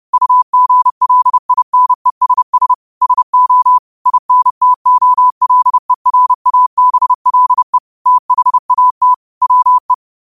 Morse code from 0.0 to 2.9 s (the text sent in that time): UGLINESS